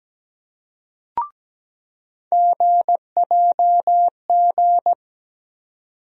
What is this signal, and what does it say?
Morse code, heard as a keyed tone, about 17 wpm, 700 Hz. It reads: GJG
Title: USA Call Sign Postfixes - 3 Letters 17wpm